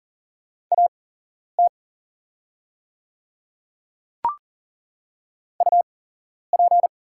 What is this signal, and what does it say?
Morse code, heard as a keyed tone, about 40 wpm, spaced as for 11 wpm, 700 Hz. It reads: AT UP
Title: 8x Character Spacing - Top 100 Words - 40wpm